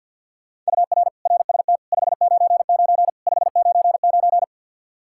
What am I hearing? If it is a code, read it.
Morse code, heard as a keyed tone, about 52 wpm, 700 Hz. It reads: UR RST 599 599
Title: Practice QSO - Exchange 1 - Beginning - 50wpm